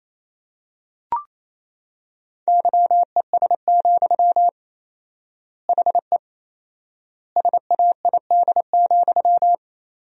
Morse code, heard as a keyed tone, about 28 wpm, 700 Hz. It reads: YES, HE SAID,
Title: 4x Word-Spacing - Sentences from Top 500 Words - 28wpm